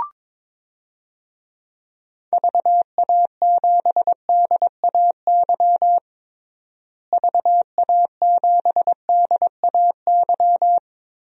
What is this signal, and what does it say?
Morse code, heard as a keyed tone, about 22 wpm, 700 Hz. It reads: VA7DAY VA7DAY
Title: DXCC-100 Call Signs - 6 Characters - Repeated 3x - 22wpm